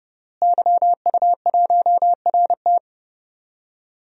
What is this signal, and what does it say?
Morse code, heard as a keyed tone, about 30 wpm, 700 Hz. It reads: YU1RT
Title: DXCC-100 Call Signs - 5 Characters - Repeated 3x - 30wpm